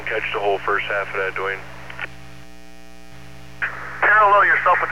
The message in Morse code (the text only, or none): none